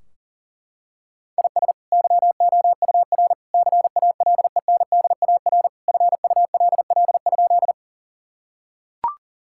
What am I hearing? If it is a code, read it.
Morse code, heard as a keyed tone, about 40 wpm, 700 Hz. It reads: IS YOUR CALENDAR FULL?
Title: Sentences from Top 1600 Words 40wpm